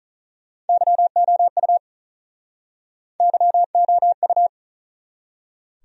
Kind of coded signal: Morse code

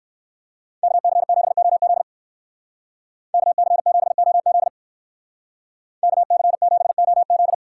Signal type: Morse code

